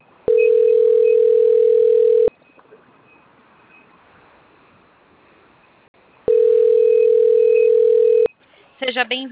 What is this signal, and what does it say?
Ringback tone (North America), 440 + 480 Hz, with speech.
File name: out- - - - - .4869